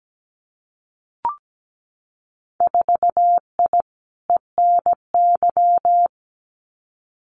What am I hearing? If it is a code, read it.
Morse code, heard as a keyed tone, about 17 wpm, 700 Hz. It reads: 4I ENY
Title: Winter Field Day Exchanges 17wpm